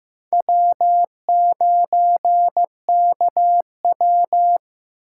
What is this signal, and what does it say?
Morse code, heard as a keyed tone, about 18 wpm, 700 Hz. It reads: W9KW
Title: Speed Racing - USA 1x2 Call Signs 15wpm